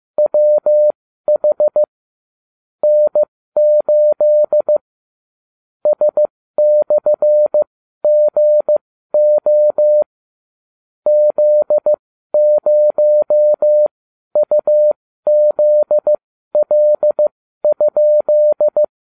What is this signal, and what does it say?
Morse code, heard as a keyed tone, about 15 wpm, spaced as for 12 wpm, 600 Hz. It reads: WH N8 S/GO Z0UZL?